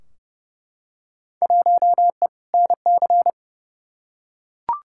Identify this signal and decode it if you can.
Morse code, heard as a keyed tone, about 31 wpm, 700 Hz. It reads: 1E NC
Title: Field Day Exchanges - 30wpm